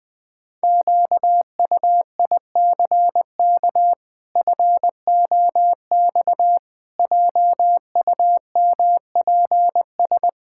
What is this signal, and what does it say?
Morse code, heard as a keyed tone, about 20 wpm, 700 Hz. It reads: QUICK FOX JUMPS
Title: Pangram Challenge v2 - 20wpm